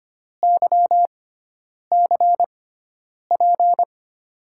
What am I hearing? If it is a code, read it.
Morse code, heard as a keyed tone, about 26 wpm, spaced as for 9 wpm, 700 Hz. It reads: YCP
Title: USA Call Signs - 6 Characters - 6x Character Spacing - 25wpm